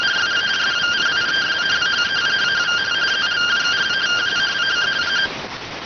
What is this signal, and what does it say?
Two-tone teleprinter signal, about 170 Hz shift.